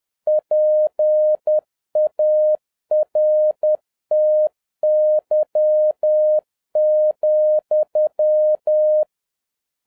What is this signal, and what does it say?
Morse code, heard as a keyed tone, about 10 wpm, 610 Hz. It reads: PARTY,
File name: 2011-07-mp3 / n31 THE 3.11 2 10wpm